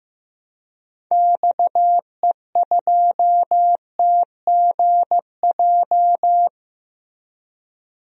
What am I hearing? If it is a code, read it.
Morse code, heard as a keyed tone, about 15 wpm, 700 Hz. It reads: XE2TGJ
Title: DXCC-100 Call Signs - 6 Characters - Repeated 2x - 15wpm